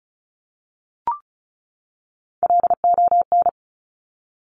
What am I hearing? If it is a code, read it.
Morse code, heard as a keyed tone, about 36 wpm, 700 Hz. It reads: LON
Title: Top Three Letter Combinations in Words 35wpm